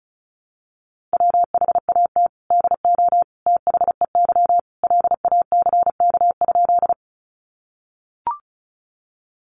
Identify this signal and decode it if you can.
Morse code, heard as a keyed tone, about 34 wpm, 700 Hz. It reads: WHAT DO THEY LACK?
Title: Sentences from Top 1100 Words 35wpm